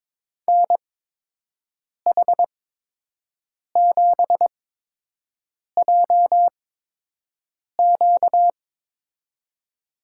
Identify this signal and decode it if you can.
Morse code, heard as a keyed tone, about 22 wpm, spaced as for 6 wpm, 700 Hz. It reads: NH7JQ
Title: USA Call Signs - 8x Character Spacing - 22wpm